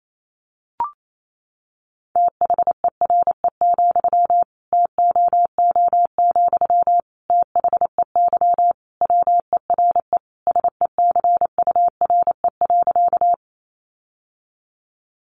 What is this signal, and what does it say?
Morse code, heard as a keyed tone, about 28 wpm, 700 Hz. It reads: THERE, TOO, THEY WERE SECURE.
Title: Sentences from Top 1200 Words 28wpm